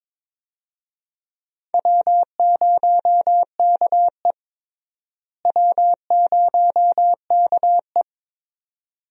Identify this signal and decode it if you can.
Morse code, heard as a keyed tone, about 22 wpm, 700 Hz. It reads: W0KE W0KE
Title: USA Call Signs - 4 Characters - Repeated 2x - 22wpm